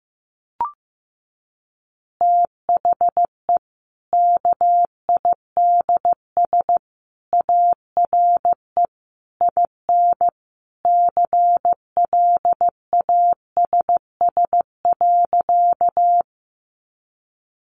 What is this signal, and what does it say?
Morse code, heard as a keyed tone, about 15 wpm, 700 Hz. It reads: THE KIDS ARE IN CLASS.
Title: Easy Phrases 15wpm